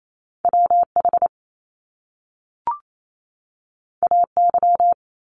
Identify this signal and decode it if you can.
Morse code, heard as a keyed tone, about 28 wpm, 700 Hz. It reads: WH AY
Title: Top Two Letter Combinations in Words 28wpm